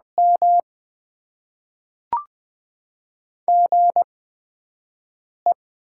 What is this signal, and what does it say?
Morse code, heard as a keyed tone, about 20 wpm, spaced as for 5 wpm, 700 Hz. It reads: M GE
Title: Top 20 QSO Elements - 8x Character Spacing - 20wpm